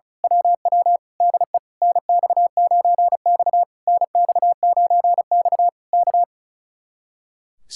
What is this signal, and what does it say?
Morse code, heard as a keyed tone, about 33 wpm, 700 Hz. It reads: WW DE NX9X NX9X K
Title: Speed-Racing Common QSO Phrases - 35wpm